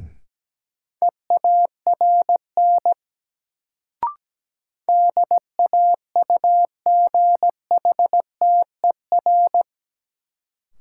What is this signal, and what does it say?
Morse code, heard as a keyed tone, about 17 wpm, 700 Hz. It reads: EARN DAUGHTER
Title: Sentences from Top 1500 Words - New Word Drill - 17wpm